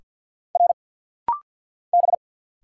Morse code, heard as a keyed tone, about 50 wpm, 700 Hz. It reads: R B
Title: Single Letters - Rapid-Fire 50wpm